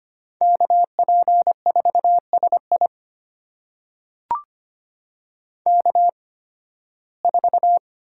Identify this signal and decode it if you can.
Morse code, heard as a keyed tone, about 25 wpm, spaced as for 16 wpm, 700 Hz. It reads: KP4SI K4
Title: USA Call Signs - 8x Character Spacing - 25wpm